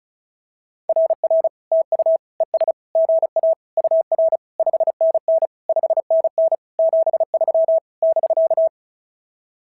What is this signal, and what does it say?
Morse code, heard as a keyed tone, about 36 wpm, 650 Hz. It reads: RR TU ES GA UR 5NN 5NN 73 <BK>
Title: SOTA QSO Practice 35wpm